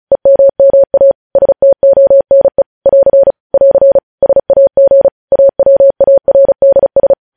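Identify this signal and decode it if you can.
Morse code, heard as a keyed tone, about 36 wpm, 550 Hz. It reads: EMMA STONE <AR> <AR> SAG AWARDS
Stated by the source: Morse code news headlines practice sent at 35 wpm.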